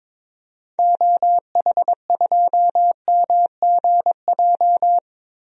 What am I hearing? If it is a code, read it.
Morse code, heard as a keyed tone, about 21 wpm, 700 Hz. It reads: OH2MGJ
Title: DXCC-100 Call Signs - 6 Characters - Repeated 3x - 22wpm